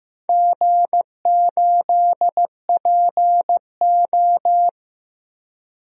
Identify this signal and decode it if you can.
Morse code, heard as a keyed tone, about 15 wpm, 700 Hz. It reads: G8PO
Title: DXCC-100 Call Signs - 4 Characters - Repeated 3x - 15wpm